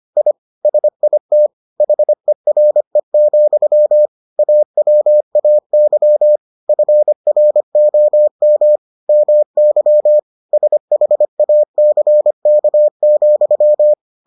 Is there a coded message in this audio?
Yes: Morse code from 0.2 to 13.9 s, heard as a keyed tone, about 25 wpm, 600 Hz. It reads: I SIT HERE, AWAY FROM MY SHACK,